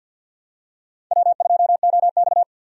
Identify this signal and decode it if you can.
Morse code, heard as a keyed tone, about 49 wpm, 700 Hz. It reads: W2OX
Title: USA Call Signs - 4 Characters - 50wpm